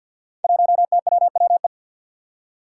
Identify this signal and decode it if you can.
Morse code, heard as a keyed tone, about 49 wpm, 700 Hz. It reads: 1TWWE
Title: Five Challenging Letter-Number Combinations 50wpm